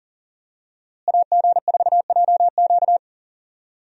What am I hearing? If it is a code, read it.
Morse code, heard as a keyed tone, about 39 wpm, 700 Hz. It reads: AG4JQ
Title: USA Call Signs - 5 Characters - Repeated 2x - 40wpm